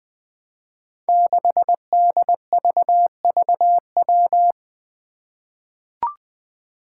Morse code, heard as a keyed tone, about 20 wpm, 700 Hz. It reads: 6DVVW
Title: Five Challenging Letter-Number Combinations 20wpm